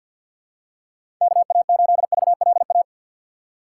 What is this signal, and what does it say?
Morse code, heard as a keyed tone, about 52 wpm, 700 Hz. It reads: KA8VLA
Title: USA Call Signs - 6 Characters - 4x Character Spacing - 50wpm